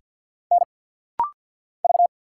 Morse code, heard as a keyed tone, about 50 wpm, 700 Hz. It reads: N V
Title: Single Letters - Rapid-Fire 50wpm